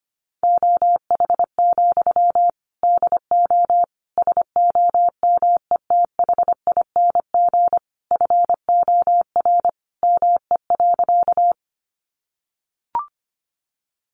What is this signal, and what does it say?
Morse code, heard as a keyed tone, about 25 wpm, 700 Hz. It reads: OH, DO SOMETHING FOR ME.
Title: Sentences from Top 600 Words 25wpm